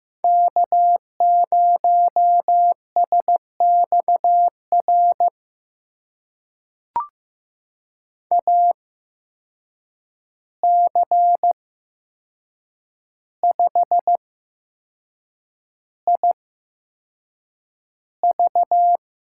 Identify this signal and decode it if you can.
Morse code, heard as a keyed tone, about 15 wpm, spaced as for 6 wpm, 700 Hz. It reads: K0SXR AC5IV
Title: USA Call Signs - 5 Characters - 8x Character Spacing - 15wpm